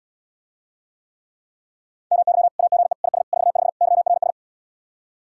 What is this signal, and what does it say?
Morse code, heard as a keyed tone, about 79 wpm, 700 Hz. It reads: MY AGE IS 45 YRS